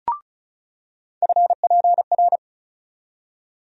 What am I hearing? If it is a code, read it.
Morse code, heard as a keyed tone, about 36 wpm, 700 Hz. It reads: FPR
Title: USA Call Sign Postfixes - 3 Letters 35wpm